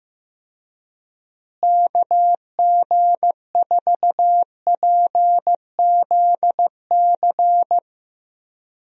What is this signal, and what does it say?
Morse code, heard as a keyed tone, about 16 wpm, 700 Hz. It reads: KG4PZC
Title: USA Call Signs - 6 Characters - Repeated 3x - 15wpm